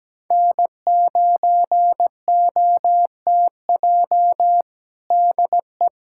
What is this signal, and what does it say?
Morse code, heard as a keyed tone, about 17 wpm, 700 Hz. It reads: N9OTJ DE
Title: USA Callsign DE Callsign - 5 Characters - 17wpm